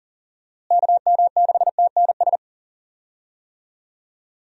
Morse code, heard as a keyed tone, about 40 wpm, 700 Hz. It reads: KM6TNS
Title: USA Call Signs - 6 Characters - Repeated 2x - 40wpm